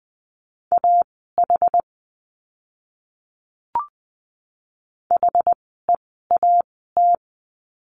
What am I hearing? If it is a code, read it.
Morse code, heard as a keyed tone, about 20 wpm, spaced as for 15 wpm, 700 Hz. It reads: AH HEAT
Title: 061 Lesson 11f - Letter h - Words 20wpm